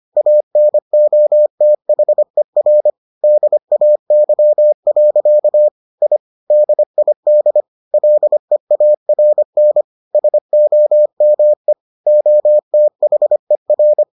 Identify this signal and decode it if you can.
Morse code, heard as a keyed tone, about 25 wpm, 600 Hz. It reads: ANOTHER DAY. I DID LEARN SOME OTHER